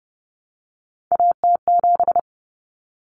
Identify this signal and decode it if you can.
Morse code, heard as a keyed tone, about 30 wpm, 700 Hz. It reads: AT7
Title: Common Cut Numbers - 3-Digit 30wpm